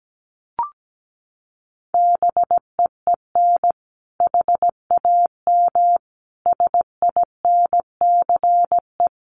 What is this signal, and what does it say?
Morse code, heard as a keyed tone, about 17 wpm, 700 Hz. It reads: BEEN HAM SINCE
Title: Common QSO Phrases 17wpm